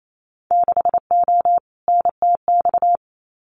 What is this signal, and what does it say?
Morse code, heard as a keyed tone, about 28 wpm, 700 Hz. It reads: 6O NTX
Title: Winter Field Day Exchanges Encore 28wpm